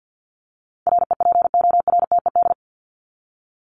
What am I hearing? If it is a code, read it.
Morse code, heard as a keyed tone, about 50 wpm, 700 Hz. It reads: REPORTED
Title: ED Suffix - Top 100 Words with Base Words 50wpm